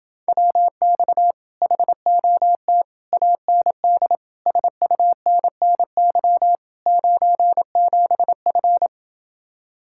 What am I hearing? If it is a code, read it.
Morse code, heard as a keyed tone, about 27 wpm, 700 Hz. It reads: WX HOT AND SUNNY 97F